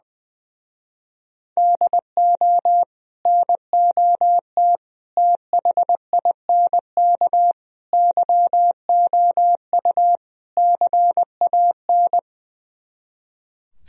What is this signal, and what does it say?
Morse code, heard as a keyed tone, about 21 wpm, 700 Hz. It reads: DO NOT THINK YOU CAN
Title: Speed-Racing - Sets of 5 Words 20wpm